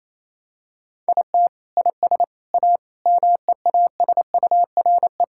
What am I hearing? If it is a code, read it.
Morse code, heard as a keyed tone, about 27 wpm, 700 Hz. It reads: IT IS A MEASURE
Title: Sentences from Top 1600 Words 28wpm